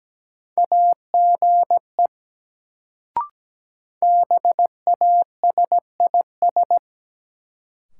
Morse code, heard as a keyed tone, about 17 wpm, 700 Hz. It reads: AGE BASIS
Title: Sets of 3 Words - New Word Drill - 17wpm